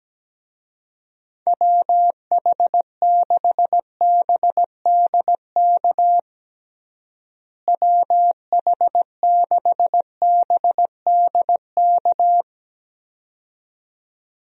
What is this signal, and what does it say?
Morse code, heard as a keyed tone, about 17 wpm, 700 Hz. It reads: WH6BDK WH6BDK
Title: USA Call Signs - 6 Characters - Repeated 2x - 17wpm